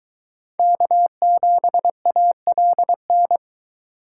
Morse code, heard as a keyed tone, about 22 wpm, 700 Hz. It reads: K7ALN